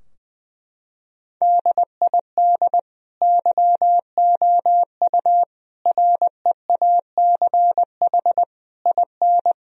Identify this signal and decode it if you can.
Morse code, heard as a keyed tone, about 20 wpm, 700 Hz. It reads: DID YOU REACH IN
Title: Sentences from Top 1000 Words - Review Part 3 - 20wpm